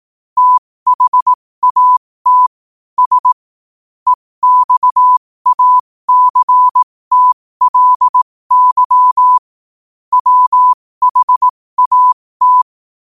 Morse code, heard as a keyed tone, about 18 wpm, spaced as for 15 wpm, 1000 Hz. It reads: THAT S EXACTLY WHAT